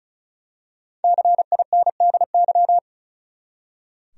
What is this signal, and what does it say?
Morse code, heard as a keyed tone, about 34 wpm, 700 Hz. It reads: CINDY
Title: USA Common Names - Max 5 Letters - 35wpm